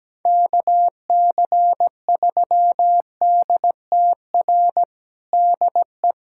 Morse code, heard as a keyed tone, about 17 wpm, 700 Hz. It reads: KC3DTR DE
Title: USA Callsign DE Callsign 17wpm